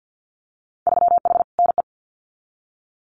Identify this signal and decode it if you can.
Morse code, heard as a keyed tone, about 50 wpm, 700 Hz. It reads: 3H NE
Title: Winter Field Day Exchanges Encore 50wpm